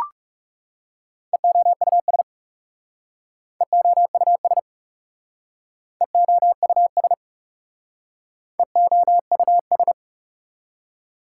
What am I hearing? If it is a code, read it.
Morse code, heard as a keyed tone, about 40 wpm, 700 Hz. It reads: EOUS EOUS EOUS EOUS